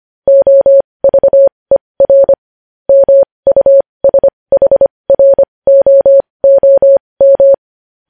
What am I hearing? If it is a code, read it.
Morse code, heard as a keyed tone, about 25 wpm, spaced as for 20 wpm, 550 Hz. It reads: OVER MUSHROOM
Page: news headlines in morse code at 20 wpm